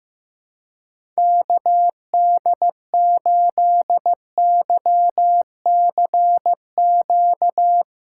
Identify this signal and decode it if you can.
Morse code, heard as a keyed tone, about 15 wpm, 700 Hz. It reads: KD8YCQ
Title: USA Call Signs - 6 Characters - 3x Character Spacing - 15wpm